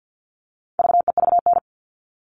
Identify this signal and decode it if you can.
Morse code, heard as a keyed tone, about 50 wpm, 700 Hz. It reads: VEVN
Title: All Cut Numbers - 4-Digit 50wpm